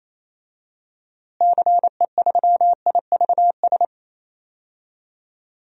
Morse code, heard as a keyed tone, about 28 wpm, 700 Hz. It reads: CE3IVS
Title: DXCC-100 Call Signs - 6 Characters - Repeated 2x - 28wpm